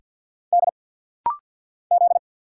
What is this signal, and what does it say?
Morse code, heard as a keyed tone, about 50 wpm, 690 Hz. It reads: D Z
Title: Single Letter-Number Alt - Rapid-Fire 50wpm